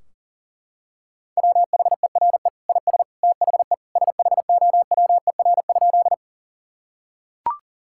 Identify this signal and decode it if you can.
Morse code, heard as a keyed tone, about 40 wpm, 700 Hz. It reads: WHERE IS THE SHOWER?
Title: Sentences from Top 1600 Words 40wpm